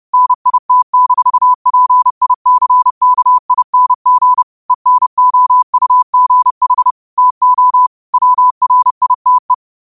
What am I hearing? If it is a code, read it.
Morse code, heard as a keyed tone, about 30 wpm, 1000 Hz. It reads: NIT-PICKING ENOUGH TO WRITE
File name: qotdfiles / 30wpm